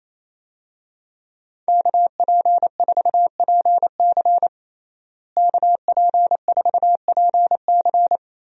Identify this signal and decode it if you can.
Morse code, heard as a keyed tone, about 28 wpm, 700 Hz. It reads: KP4PC KP4PC
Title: DXCC-100 Call Signs - 5 Characters - Repeated 3x - 28wpm